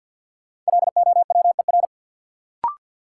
Morse code, heard as a keyed tone, about 51 wpm, 700 Hz. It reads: LOWER
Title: Sentences from Top 700 Words - New Word Drill - 50wpm